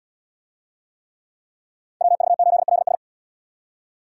Morse code, heard as a keyed tone, about 77 wpm, 700 Hz. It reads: WV8LS